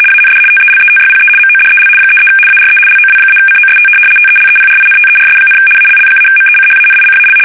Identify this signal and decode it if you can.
RTTY, about 75 baud. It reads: AND TELETYPE MODEL 33 ARE LOW-COST TELEPRINTERS, ALL-MECHANICAL
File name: fsk850-75B / teletype-corp